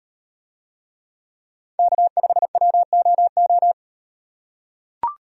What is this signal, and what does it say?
Morse code, heard as a keyed tone, about 37 wpm, 700 Hz. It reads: K5WOO